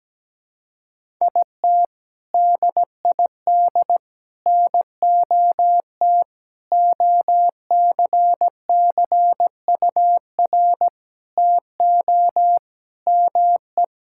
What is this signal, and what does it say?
Morse code, heard as a keyed tone, about 17 wpm, 700 Hz. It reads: IT DID NOT OCCUR TO ME
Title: Sentences from Top 1000 Words - Review Part 6 - 17wpm